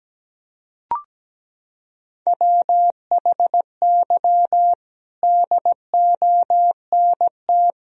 Morse code, heard as a keyed tone, about 17 wpm, 700 Hz. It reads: WHY DONT
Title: Sentences from Top 1000 Words - Review Part 4 - 17wpm